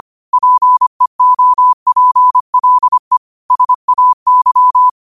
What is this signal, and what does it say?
Morse code, heard as a keyed tone, about 25 wpm, 1000 Hz. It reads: PEOPLE SAY
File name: qotdfiles / 25wpm